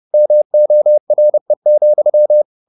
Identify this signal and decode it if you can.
Morse code, heard as a keyed tone, about 30 wpm, 600 Hz. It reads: MORE,
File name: the old man and the sea / the old man and the sea 1 30wpm